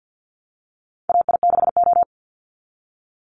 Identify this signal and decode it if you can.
Morse code, heard as a keyed tone, about 50 wpm, 700 Hz. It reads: AI6O
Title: Call Signs Repeated 3x - 50wpm